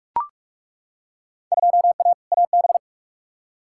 Morse code, heard as a keyed tone, about 42 wpm, 700 Hz. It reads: 2A AB
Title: Field Day Exchanges - 45wpm